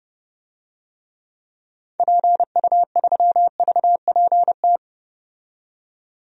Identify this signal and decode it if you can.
Morse code, heard as a keyed tone, about 30 wpm, 700 Hz. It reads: PU3VPT
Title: DXCC-100 Call Signs - 6 Characters - 30wpm